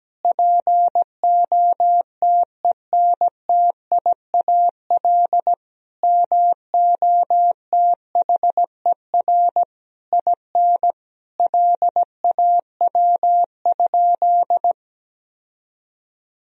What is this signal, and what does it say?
Morse code, heard as a keyed tone, about 17 wpm, 700 Hz. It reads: POTENTIAL MOTHER IN LAW?
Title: Sentences from Top 1000 Words - Review Part 4 - 17wpm